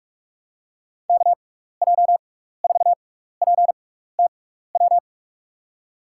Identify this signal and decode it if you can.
Morse code, heard as a keyed tone, about 45 wpm, spaced as for 15 wpm, 700 Hz. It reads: KJ4PTW
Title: USA Call Signs - 6 Characters - 6x Character Spacing - 45wpm